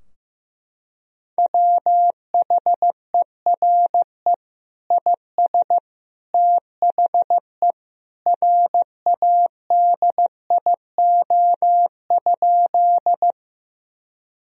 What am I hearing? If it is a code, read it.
Morse code, heard as a keyed tone, about 14 wpm, 700 Hz. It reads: WHERE IS THE RADIO?
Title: Sentences from Top 1000 Words - Review Part 3 - 15wpm